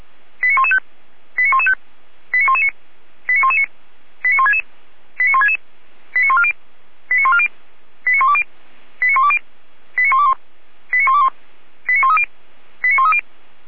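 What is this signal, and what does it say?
Five-tone selective call (ZVEI), heard as quick runs of five tones, tones of about 80 ms. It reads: 89115 89115 89119 89119 89166 89166 89140 89140 89120 89120 89121 89121 89130 89130
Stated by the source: ZVEI five-tone selective call 89110 received by SDR.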